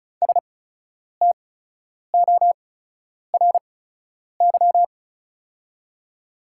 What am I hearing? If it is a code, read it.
Morse code, heard as a keyed tone, about 36 wpm, spaced as for 10 wpm, 690 Hz. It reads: STORY
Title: 8x Character Spacing - Top 500 Words - 35wpm